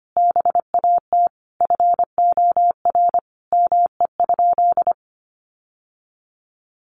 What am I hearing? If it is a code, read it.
Morse code, heard as a keyed tone, about 25 wpm, 700 Hz. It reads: BAT FOR ME?
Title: Sentences from Top 1500 Words 25wpm